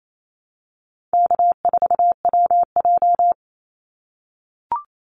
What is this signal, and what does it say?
Morse code, heard as a keyed tone, about 28 wpm, 700 Hz. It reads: K4WJ
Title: Call Signs - 28wpm